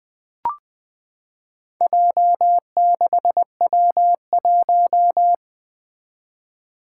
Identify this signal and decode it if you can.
Morse code, heard as a keyed tone, about 20 wpm, 700 Hz. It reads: J6W1
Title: Four Challenging Letter-Number Combinations 20wpm